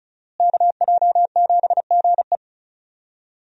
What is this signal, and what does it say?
Morse code, heard as a keyed tone, about 35 wpm, 700 Hz. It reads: KJ7GE
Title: USA Call Signs - 5 Characters - Repeated 3x - 35wpm